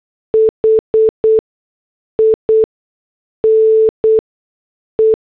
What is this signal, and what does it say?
Morse code, heard as a keyed tone, about 8 wpm, 440 Hz. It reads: H I N E
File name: morse / headlines